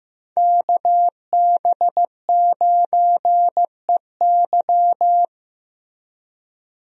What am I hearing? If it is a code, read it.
Morse code, heard as a keyed tone, about 15 wpm, 700 Hz. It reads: KB9EY
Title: USA Call Signs - 5 Characters - Repeated 2x - 15wpm